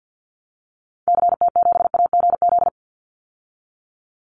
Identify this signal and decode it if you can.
Morse code, heard as a keyed tone, about 50 wpm, 700 Hz. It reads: CT7AGZ
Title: Call Signs Repeated 2x - 50wpm